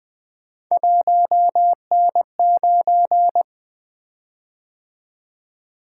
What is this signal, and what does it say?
Morse code, heard as a keyed tone, about 20 wpm, 700 Hz. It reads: 1N9
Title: Three Letter-Number Combinations - 20wpm